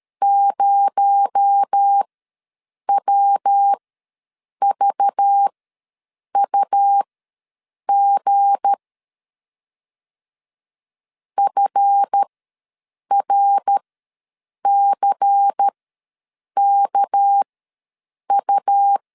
Morse code, heard as a keyed tone, about 13 wpm, spaced as for 7 wpm, 800 Hz. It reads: 0WVUG FRCKU
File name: LEZIONI / 7 GF9Z / 7 GF ws 12wpm 3140